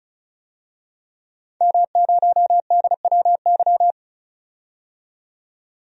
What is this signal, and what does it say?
Morse code, heard as a keyed tone, about 36 wpm, 700 Hz. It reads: M0DWY
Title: DXCC-100 Call Signs - 5 Characters - 35wpm